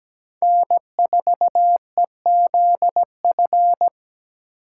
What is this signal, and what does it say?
Morse code, heard as a keyed tone, about 20 wpm, 700 Hz. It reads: N4EZF
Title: Speed Racing - USA 1x3 Call Signs 17wpm